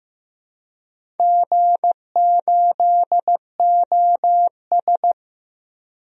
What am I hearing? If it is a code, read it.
Morse code, heard as a keyed tone, about 15 wpm, 700 Hz. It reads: G8OS
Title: DXCC-100 Call Signs - 4 Characters - Repeated 2x - 15wpm